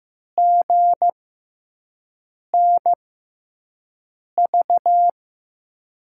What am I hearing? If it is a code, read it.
Morse code, heard as a keyed tone, about 15 wpm, spaced as for 5 wpm, 700 Hz. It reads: GNV
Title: USA Call Signs - 6 Characters - 6x Character Spacing - 15wpm